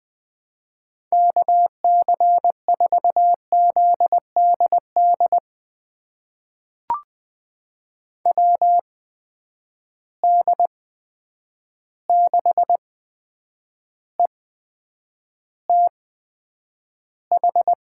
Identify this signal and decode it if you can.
Morse code, heard as a keyed tone, about 20 wpm, spaced as for 9 wpm, 700 Hz. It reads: KC4ZDD WD6ETH
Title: USA Call Signs - 6 Characters - 8x Character Spacing - 20wpm